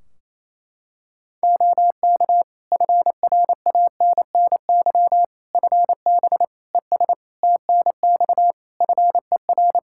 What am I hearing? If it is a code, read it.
Morse code, heard as a keyed tone, about 28 wpm, 700 Hz. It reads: OK FRANNY FB ES TNX FER
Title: Practice QSO - Exchange 2 - Info - 28wpm